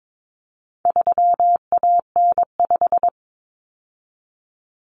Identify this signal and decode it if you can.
Morse code, heard as a keyed tone, about 22 wpm, 700 Hz. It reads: 3AN5
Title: Common Cut Numbers - 4-Digit 22wpm